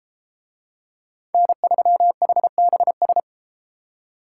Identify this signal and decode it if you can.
Morse code, heard as a keyed tone, about 32 wpm, 700 Hz. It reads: N3HBS